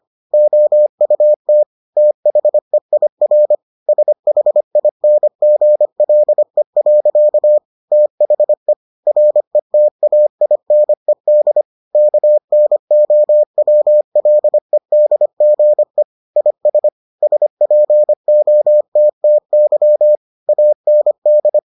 Morse code, heard as a keyed tone, about 25 wpm, 600 Hz. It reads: OUT THEIR SHINGLE. THE RETAINED KNOWLEDGE IS SPOTTY AND